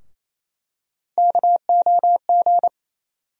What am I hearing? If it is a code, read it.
Morse code, heard as a keyed tone, about 28 wpm, 700 Hz. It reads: KOG
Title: USA Call Sign Postfixes - 3 Letters 28wpm